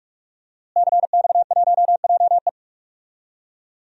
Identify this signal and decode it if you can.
Morse code, heard as a keyed tone, about 44 wpm, 700 Hz. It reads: CX1JE
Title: DXCC-100 Call Signs - 5 Characters - Repeated 3x - 45wpm